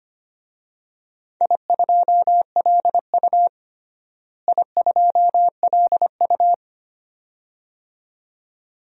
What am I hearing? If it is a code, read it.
Morse code, heard as a keyed tone, about 25 wpm, 700 Hz. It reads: I2LU I2LU
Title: DXCC-100 Call Signs - 4 Characters - Repeated 2x - 25wpm